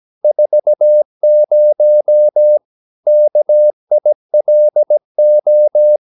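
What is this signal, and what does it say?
Morse code, heard as a keyed tone, about 17 wpm, 600 Hz. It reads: 40 KILO
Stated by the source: Slow (QRS) Morse practice at 17 wpm, from CW med Gnister.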